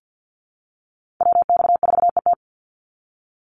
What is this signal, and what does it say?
Morse code, heard as a keyed tone, about 48 wpm, 700 Hz. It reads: WX4ET
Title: NA SOTA Chaser Callsigns 50wpm